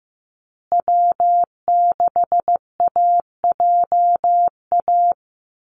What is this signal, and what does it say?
Morse code, heard as a keyed tone, about 15 wpm, 700 Hz. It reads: W6AJA
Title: USA Call Signs Repeated 2x 15wpm